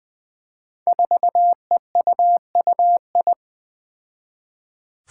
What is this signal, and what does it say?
Morse code, heard as a keyed tone, about 20 wpm, 700 Hz. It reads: 4EUUI
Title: Five Challenging Letter-Number Combinations 20wpm